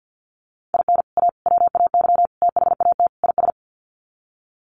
Morse code, heard as a keyed tone, about 50 wpm, 700 Hz. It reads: IN A WAY THAT IS
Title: Sets of 5 Words Encore 50wpm